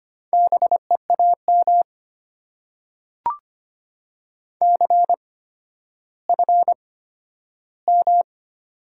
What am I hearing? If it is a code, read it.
Morse code, heard as a keyed tone, about 25 wpm, spaced as for 12 wpm, 690 Hz. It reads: BEAM CFM
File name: Top 90 QSO Elements - 8x Character Spacing - 25wpm